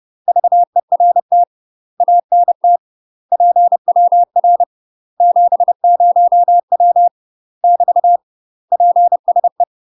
Morse code, heard as a keyed tone, about 30 wpm, 700 Hz. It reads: VERT ANT PWR 70W <BT> PSE
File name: UK MASTER QSOs TRAINING / 30WPM / UK MASTER QSOs 0042